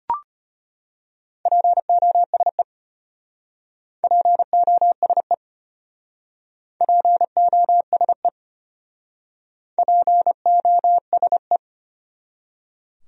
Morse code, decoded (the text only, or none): POSE POSE POSE POSE